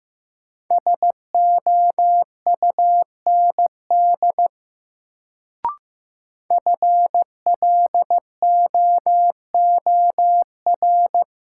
Morse code, heard as a keyed tone, about 15 wpm, 700 Hz. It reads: SOUND FLOOR
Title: Sets of 3 Words - New Word Drill - 15wpm